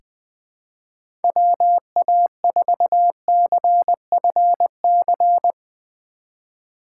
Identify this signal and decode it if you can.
Morse code, heard as a keyed tone, about 20 wpm, 700 Hz. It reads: WA4CFC
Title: USA Call Signs - 6 Characters - 8x Character Spacing - 20wpm